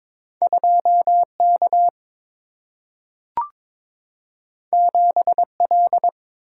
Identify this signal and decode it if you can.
Morse code, heard as a keyed tone, about 22 wpm, 700 Hz. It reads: 2 K 7 L
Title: Single-Letter Single-Number 22wpm